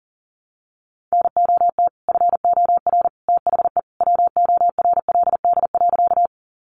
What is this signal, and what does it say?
Morse code, heard as a keyed tone, about 40 wpm, 700 Hz. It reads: NOT FOR THE WORLD.
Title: Sentences from Top 600 Words 40wpm